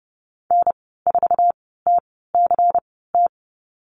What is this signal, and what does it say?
Morse code, heard as a keyed tone, about 30 wpm, spaced as for 17 wpm, 700 Hz. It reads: N4TCT
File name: Lesson 19g - Letter w - Call Signs 30wpm